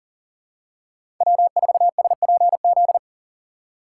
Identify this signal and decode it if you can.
Morse code, heard as a keyed tone, about 40 wpm, 700 Hz. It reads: W4SPZ
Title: USA Call Signs - 5 Characters - 3x Character Spacing - 40wpm